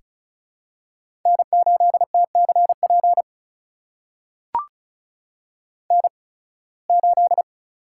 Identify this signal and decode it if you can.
Morse code, heard as a keyed tone, about 34 wpm, spaced as for 23 wpm, 700 Hz. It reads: N8TCP N8
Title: USA Call Signs - 5 Characters - 8x Character Spacing - 35wpm